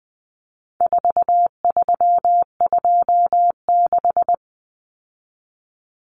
Morse code, heard as a keyed tone, about 20 wpm, 700 Hz. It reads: 4326
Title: Common Cut Numbers - 4-Digit 20wpm